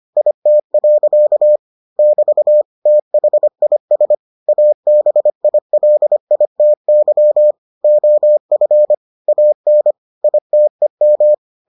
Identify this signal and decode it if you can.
Morse code, heard as a keyed tone, about 25 wpm, 600 Hz. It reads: IT. <BT> THIS ABILITY OF AN ITEM